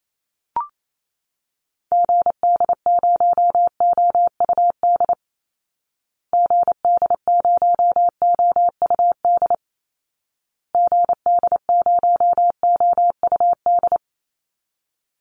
Morse code, heard as a keyed tone, about 28 wpm, 700 Hz. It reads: GD0OUD GD0OUD GD0OUD
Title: Call Signs Repeated 2x - 28wpm